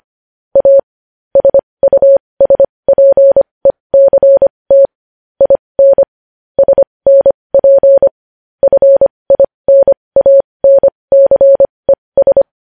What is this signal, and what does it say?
Morse code, heard as a keyed tone, about 25 wpm, spaced as for 20 wpm, 550 Hz. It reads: A SUSPECT IN SNP FINANCES
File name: podcast / News headlines-1x-25x20